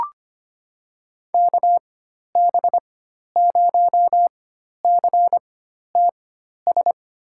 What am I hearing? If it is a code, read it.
Morse code, heard as a keyed tone, about 25 wpm, spaced as for 12 wpm, 700 Hz. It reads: KB0CTS
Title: USA Call Signs - 4x Character Spacing - 25wpm